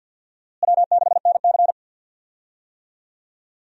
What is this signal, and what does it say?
Morse code, heard as a keyed tone, about 52 wpm, 700 Hz. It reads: W6NC
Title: USA Call Signs - 4 Characters - Repeated 3x - 50wpm